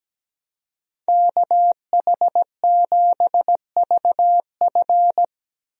Morse code, heard as a keyed tone, about 17 wpm, 700 Hz. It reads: KH7VF
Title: USA Call Signs - 5 Characters - Repeated 2x - 17wpm